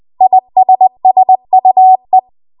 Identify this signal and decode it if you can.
Morse code, heard as a keyed tone, about 20 wpm, 750 Hz. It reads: ISSUE
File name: Archive / 20 WPM / 20WPM